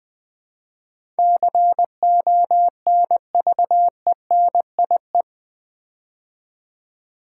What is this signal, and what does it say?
Morse code, heard as a keyed tone, about 20 wpm, 700 Hz. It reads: CONVENIE
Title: Build A Word - 4 Letters At A Time - 10-Letter Words - 20wpm